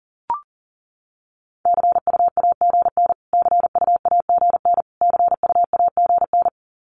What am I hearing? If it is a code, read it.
Morse code, heard as a keyed tone, about 40 wpm, 700 Hz. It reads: CUAGN CUAGN CUAGN
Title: Abbreviations Repeated 3x - 40wpm